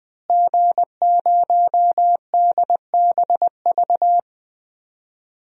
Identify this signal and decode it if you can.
Morse code, heard as a keyed tone, about 20 wpm, 700 Hz. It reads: G0DBV
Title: DXCC-100 Call Signs - 5 Characters - Repeated 2x - 20wpm